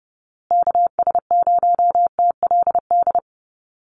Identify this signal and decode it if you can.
Morse code, heard as a keyed tone, about 30 wpm, 700 Hz. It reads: KS0TLD
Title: USA Call Signs Repeated 3x 30wpm